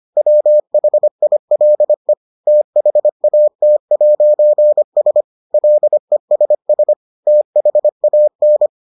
Morse code, heard as a keyed tone, about 25 wpm, 600 Hz. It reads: WHILE THAT'S LESS THAN